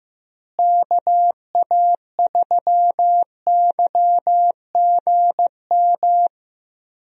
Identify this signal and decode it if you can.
Morse code, heard as a keyed tone, about 15 wpm, 700 Hz. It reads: KA3YGM
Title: USA Call Signs - 6 Characters - Repeated 3x - 15wpm